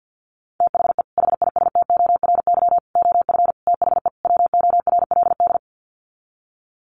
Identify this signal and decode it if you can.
Morse code, heard as a keyed tone, about 50 wpm, 700 Hz. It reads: THE HISTORY OF THE WORLD
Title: Sets of 5 Words Encore2 50wpm